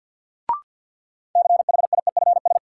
Morse code, heard as a keyed tone, about 52 wpm, 700 Hz. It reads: CHIEFS
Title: Sets of 5 Words Encore - New Word Drill - 50wpm